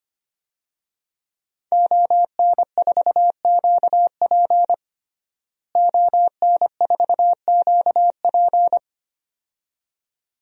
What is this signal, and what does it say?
Morse code, heard as a keyed tone, about 25 wpm, 700 Hz. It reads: ON4QP ON4QP
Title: DXCC-100 Call Signs - 5 Characters - Repeated 2x - 25wpm